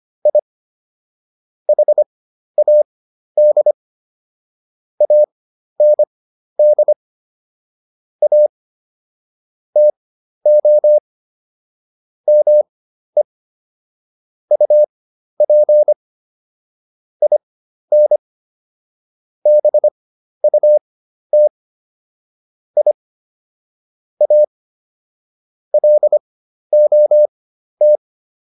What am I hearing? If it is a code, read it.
Morse code, heard as a keyed tone, about 25 wpm, spaced as for 12 wpm, 600 Hz. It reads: I HAD AND A TO ME UP IN BUT I A LOT